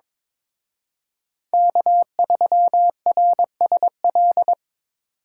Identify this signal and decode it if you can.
Morse code, heard as a keyed tone, about 26 wpm, 700 Hz. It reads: K3RSL
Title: Speed Racing - USA 1x3 Call Signs 22wpm